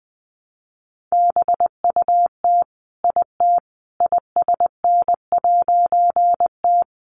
Morse code, heard as a keyed tone, about 20 wpm, 700 Hz. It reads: BUT IT ISN'T
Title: A Sound Of Thunder 20wpm